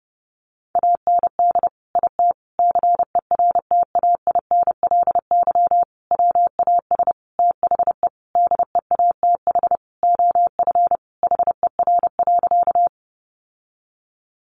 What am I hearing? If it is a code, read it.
Morse code, heard as a keyed tone, about 30 wpm, 700 Hz. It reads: AND IT CERTAINLY WAS THE DEATH OF HER.
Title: Sentences from Top 1100 Words 30wpm